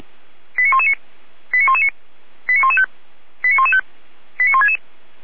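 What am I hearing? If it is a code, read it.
Five-tone selective call (ZVEI), heard as quick runs of five tones, tones of about 80 ms. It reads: 89119 89119 89115 89115 89166
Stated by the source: ZVEI five-tone selective call to address 89110, received via SDR.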